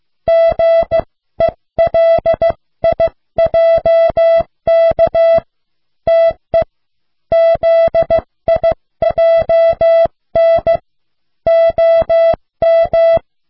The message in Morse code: GELIJK TE ZIJN OM